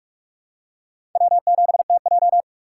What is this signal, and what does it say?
Morse code, heard as a keyed tone, about 47 wpm, 700 Hz. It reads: W7TJ
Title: USA Call Signs - 2x Character Spacing - 45wpm